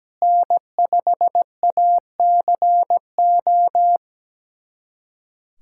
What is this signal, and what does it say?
Morse code, heard as a keyed tone, about 17 wpm, 700 Hz. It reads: N5ACO
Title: USA Callsign DE Callsign - 5 Characters - 17wpm